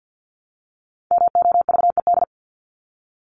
Morse code, heard as a keyed tone, about 50 wpm, 700 Hz. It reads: MOVED
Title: ED Suffix - Top 100 Words with Base Words 50wpm